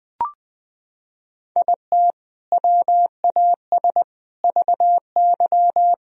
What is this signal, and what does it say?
Morse code, heard as a keyed tone, about 20 wpm, 700 Hz. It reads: IT WAS VY
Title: Sentences from Top 300 Words and Abbreviations 20wpm